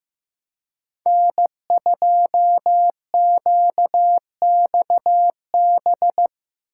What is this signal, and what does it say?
Morse code, heard as a keyed tone, about 17 wpm, 700 Hz. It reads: N2QXB
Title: Speed Racing - USA 1x3 Call Signs 15wpm